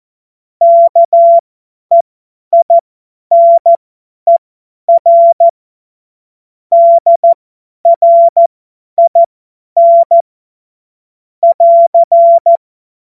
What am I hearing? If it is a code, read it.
Morse code, heard as a keyed tone, about 14 wpm, spaced as for 10 wpm, 680 Hz. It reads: KEINER DRIN <AR>
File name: Morsekurs / Witze / Witz 26. April 2022 50 F70